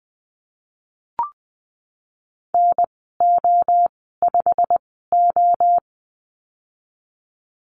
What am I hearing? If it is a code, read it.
Morse code, heard as a keyed tone, about 20 wpm, spaced as for 15 wpm, 700 Hz. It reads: NO5O
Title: 161 Lesson 26g - Letter g - Call Signs 20wpm